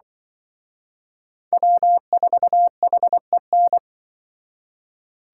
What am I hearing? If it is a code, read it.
Morse code, heard as a keyed tone, about 25 wpm, 700 Hz. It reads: W4HEN
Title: Speed Racing - USA 1x3 Call Signs 20wpm